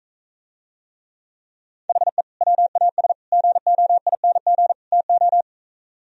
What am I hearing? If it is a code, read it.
Morse code, heard as a keyed tone, about 45 wpm, 700 Hz. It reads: HE WAS GOING TO